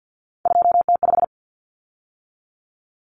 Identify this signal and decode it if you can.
Morse code, heard as a keyed tone, about 50 wpm, 700 Hz. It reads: 2T5
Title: Common Cut Numbers - 3-Digit 50wpm